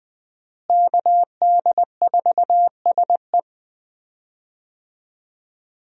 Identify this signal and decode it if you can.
Morse code, heard as a keyed tone, about 20 wpm, 700 Hz. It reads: KD4SE
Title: USA Call Signs - 5 Characters - Repeated 2x - 20wpm